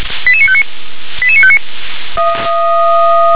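Five-tone selective call (ZVEI), heard as quick runs of five tones, tones of about 70 ms. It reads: 88959 88959
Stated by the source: ZVEI five-tone call received by SDR.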